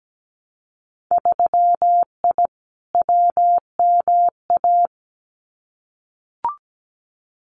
Morse code, heard as a keyed tone, about 17 wpm, 700 Hz. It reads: 3I WMA
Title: Winter Field Day Exchanges 17wpm